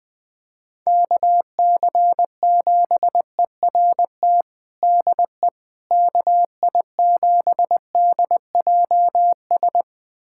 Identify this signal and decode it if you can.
Morse code, heard as a keyed tone, about 20 wpm, 700 Hz. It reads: KC7ERT DE KI7DJS
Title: USA Callsign DE Callsign - 6 Characters - 20wpm